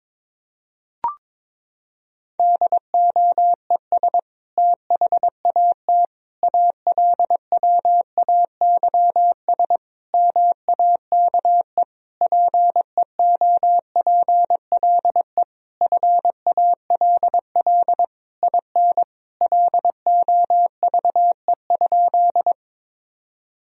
Morse code, heard as a keyed tone, about 22 wpm, 700 Hz. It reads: DOES THAT ALWAYS MAKE PEOPLE FALL IN LOVE?
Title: Sentences from Top 1000 Words - Review Part 2 - 22wpm